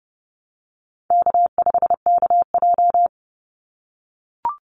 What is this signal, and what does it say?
Morse code, heard as a keyed tone, about 31 wpm, 700 Hz. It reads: K5KJ
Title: Easy Call Signs - 30wpm